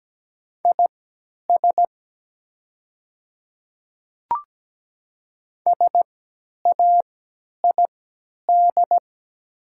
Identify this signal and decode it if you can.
Morse code, heard as a keyed tone, about 17 wpm, spaced as for 10 wpm, 700 Hz. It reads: IS SAID
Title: 3x Character Spacing - Top 100 Words - 17wpm